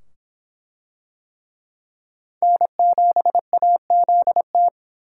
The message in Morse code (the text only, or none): N7AZT